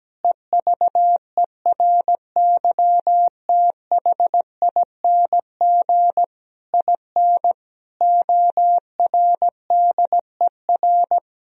Morse code, heard as a keyed tone, about 17 wpm, 700 Hz. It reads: EVERYTHING IN ORDER
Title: Sentences from Top 1000 Words - Review Part 3 - 17wpm